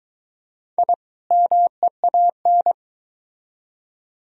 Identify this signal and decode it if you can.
Morse code, heard as a keyed tone, about 22 wpm, 700 Hz. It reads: I MEAN